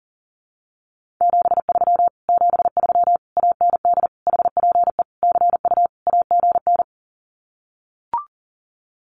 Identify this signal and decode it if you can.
Morse code, heard as a keyed tone, about 40 wpm, 700 Hz. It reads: 73 73 AND HPE CU AGN
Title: Common QSO Phrases Encore 40wpm